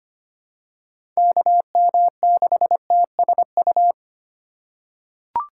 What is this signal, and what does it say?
Morse code, heard as a keyed tone, about 25 wpm, 700 Hz. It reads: KM6TSU
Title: USA Call Signs - 4x Character Spacing - 25wpm